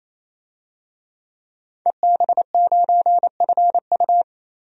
Morse code, heard as a keyed tone, about 28 wpm, 700 Hz. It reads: EB9FU
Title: DXCC-100 Call Signs - 5 Characters - Repeated 3x - 28wpm